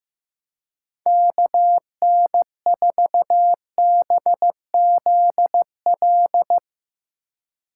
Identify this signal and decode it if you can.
Morse code, heard as a keyed tone, about 16 wpm, 700 Hz. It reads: KN4BZL
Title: Speed Racing - USA 2x3 Call Signs 15wpm